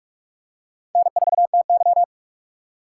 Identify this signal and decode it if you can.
Morse code, heard as a keyed tone, about 46 wpm, 700 Hz. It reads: N4TY
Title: USA Call Signs - 4 Characters - 2x Character Spacing - 45wpm